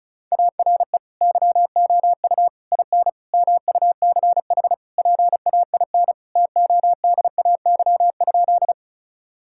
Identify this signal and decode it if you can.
Morse code, heard as a keyed tone, about 36 wpm, 700 Hz. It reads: ARE YOU IN MUCH PAIN TODAY?
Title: Sentences from Top 1600 Words 35wpm